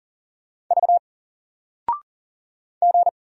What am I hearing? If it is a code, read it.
Morse code, heard as a keyed tone, about 40 wpm, 700 Hz. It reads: V G
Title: Single Letter-Number 40wpm